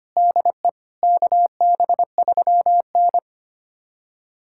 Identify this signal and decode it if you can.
Morse code, heard as a keyed tone, about 25 wpm, 700 Hz. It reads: DE KB3N
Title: USA Callsign DE Callsign - 4 Characters - 25wpm